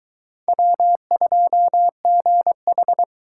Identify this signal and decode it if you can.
Morse code, heard as a keyed tone, about 23 wpm, 700 Hz. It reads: W2GH